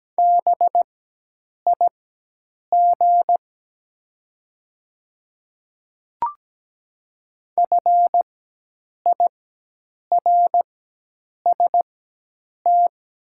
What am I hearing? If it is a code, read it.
Morse code, heard as a keyed tone, about 17 wpm, spaced as for 8 wpm, 700 Hz. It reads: BIG FIRST
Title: 4x Character Spacing - Top 200 Words - 17wpm